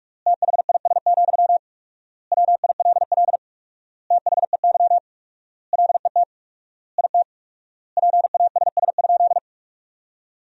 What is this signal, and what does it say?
Morse code, heard as a keyed tone, about 41 wpm, 700 Hz. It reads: THIS, WILL THEY LET IT PASS?
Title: 4x Word-Spacing - Sentences from Top 400 Words - 45wpm